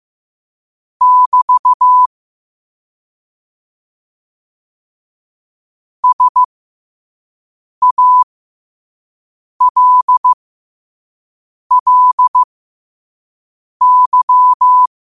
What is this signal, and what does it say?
Morse code, heard as a keyed tone, about 15 wpm, spaced as for 5 wpm, 1000 Hz. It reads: <BT> SALLY
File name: qotdfiles / 5wpm